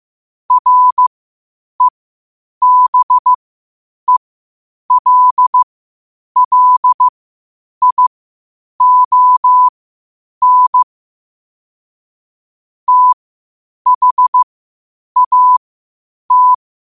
Morse code, heard as a keyed tone, about 15 wpm, spaced as for 8 wpm, 1000 Hz. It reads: REBELLION THAT
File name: qotdfiles / 8wpm